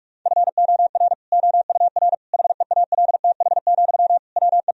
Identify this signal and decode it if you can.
Morse code, heard as a keyed tone, about 42 wpm, 700 Hz. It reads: FOR OUR HEALTH, WE
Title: Sentences from Top 1000 Words - Review Part 4 - 45wpm